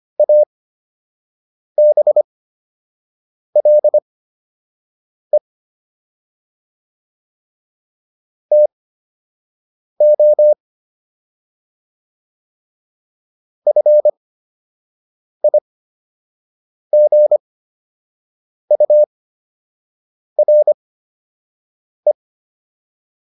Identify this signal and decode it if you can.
Morse code, heard as a keyed tone, about 25 wpm, spaced as for 6 wpm, 600 Hz. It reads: ABLE TO FIGURE